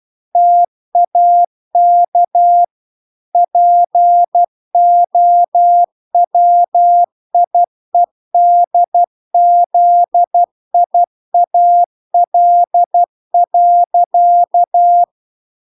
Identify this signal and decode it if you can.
Morse code, heard as a keyed tone, about 12 wpm, 700 Hz. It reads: TAK POWIEDZIAL.